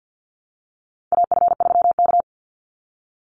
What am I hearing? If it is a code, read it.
Morse code, heard as a keyed tone, about 50 wpm, 700 Hz. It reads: AF3K
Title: Call Signs Repeated 3x - 50wpm